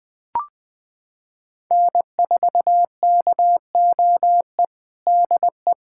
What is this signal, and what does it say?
Morse code, heard as a keyed tone, about 20 wpm, 700 Hz. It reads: N4KOE DE
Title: USA Callsign DE Callsign - 5 Characters - 20wpm